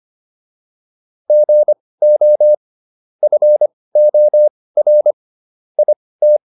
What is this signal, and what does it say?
Morse code, heard as a keyed tone, about 25 wpm, spaced as for 18 wpm, 600 Hz. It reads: GO FOR IT